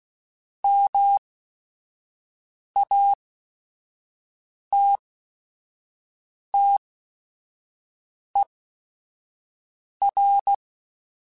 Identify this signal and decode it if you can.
Morse code, heard as a keyed tone, about 16 wpm, spaced as for 5 wpm, 800 Hz. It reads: MATTER
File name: Technical Subjects / Technical Subjects-mp3s / AND Gate @5